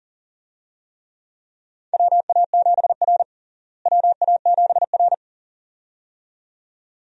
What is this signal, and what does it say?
Morse code, heard as a keyed tone, about 37 wpm, 700 Hz. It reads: WA7R WA7R
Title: USA Call Signs - 4 Characters - Repeated 2x - 40wpm